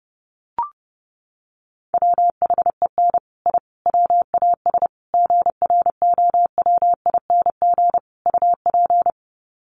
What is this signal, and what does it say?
Morse code, heard as a keyed tone, about 30 wpm, 700 Hz. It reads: WHEN I WAS GROWING UP
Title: Sets of 5 Words Encore 30wpm